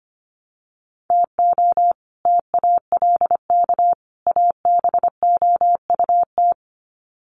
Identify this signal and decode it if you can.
Morse code, heard as a keyed tone, about 25 wpm, 700 Hz. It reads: TO TALK ABOUT
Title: Sets of 3 Words 25wpm